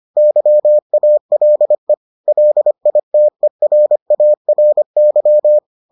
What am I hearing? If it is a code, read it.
Morse code, heard as a keyed tone, about 25 wpm, 600 Hz. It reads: YALE LITERARY